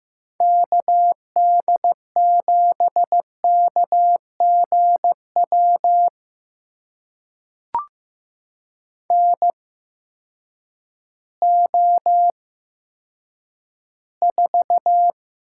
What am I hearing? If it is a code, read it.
Morse code, heard as a keyed tone, about 15 wpm, spaced as for 9 wpm, 700 Hz. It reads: KD7KGW NO4
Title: USA Call Signs - 8x Character Spacing - 15wpm